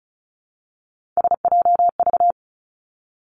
Morse code, heard as a keyed tone, about 36 wpm, 700 Hz. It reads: SJV
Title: ARRL Sections 35wpm